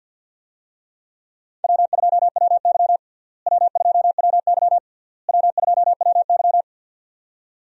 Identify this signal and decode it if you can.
Morse code, heard as a keyed tone, about 51 wpm, 700 Hz. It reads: W2WY W2WY W2WY
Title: USA Call Signs - 4 Characters - Repeated 3x - 50wpm